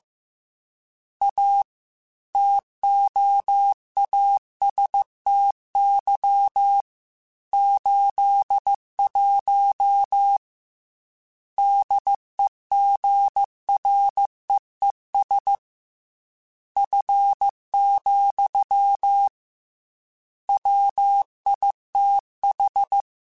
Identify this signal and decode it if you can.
Morse code, heard as a keyed tone, about 15 wpm, 800 Hz. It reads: A TOASTY 81 DEGREES F, WITH
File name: Weather Sets / Weather Sets-mp3s / Weather 5 12 @13